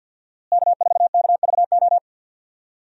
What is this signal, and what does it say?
Morse code, heard as a keyed tone, about 51 wpm, 700 Hz. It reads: K4KVO
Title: USA Call Signs - 5 Characters - Repeated 3x - 50wpm